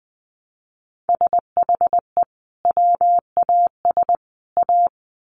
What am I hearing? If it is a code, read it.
Morse code, heard as a keyed tone, about 20 wpm, 700 Hz. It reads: SHE WAS A
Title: Sets of 3 Words 20wpm